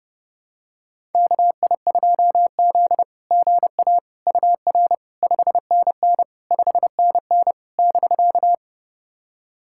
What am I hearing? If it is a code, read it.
Morse code, heard as a keyed tone, about 32 wpm, 690 Hz. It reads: KI2Z GA UR 5NN 5NN <BK>
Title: SOTA QSO Practice 30wpm